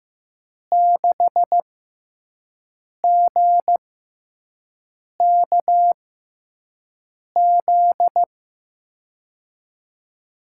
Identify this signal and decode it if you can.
Morse code, heard as a keyed tone, about 15 wpm, spaced as for 5 wpm, 700 Hz. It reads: 6GKZ
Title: USA Call Signs - 6 Characters - 6x Character Spacing - 15wpm